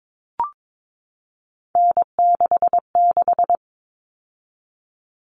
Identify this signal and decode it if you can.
Morse code, heard as a keyed tone, about 22 wpm, 700 Hz. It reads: N66
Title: All Cut Numbers - 3-Digit 22wpm